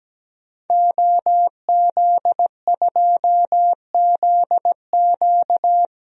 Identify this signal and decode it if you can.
Morse code, heard as a keyed tone, about 17 wpm, 700 Hz. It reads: OZ2ZQ
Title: DXCC-100 Call Signs - 5 Characters - 17wpm